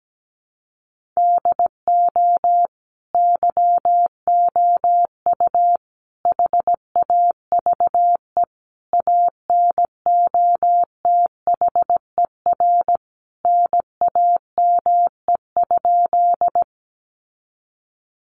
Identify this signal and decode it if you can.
Morse code, heard as a keyed tone, about 17 wpm, 700 Hz. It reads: DO YOU HAVE ANOTHER NAME?
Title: Sentences from Top 600 Words 17wpm